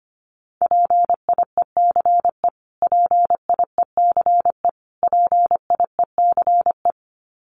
Morse code, heard as a keyed tone, about 25 wpm, 700 Hz. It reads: PIECE PIECE PIECE
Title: Top 500 Words Repeated 25wpm